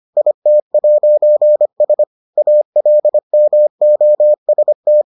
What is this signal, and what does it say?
Morse code, heard as a keyed tone, about 25 wpm, 600 Hz. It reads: IT'S ALMOST